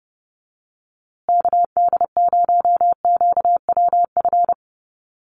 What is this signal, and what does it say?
Morse code, heard as a keyed tone, about 30 wpm, 700 Hz. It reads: KD0QWF
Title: USA Call Signs Repeated 3x 30wpm